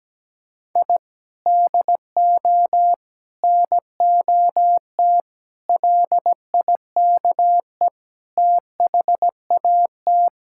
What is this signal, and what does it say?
Morse code, heard as a keyed tone, about 17 wpm, 700 Hz. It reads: I DO NOT LIKE THAT
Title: Sentences from Top 1000 Words - Review Part 6 - 17wpm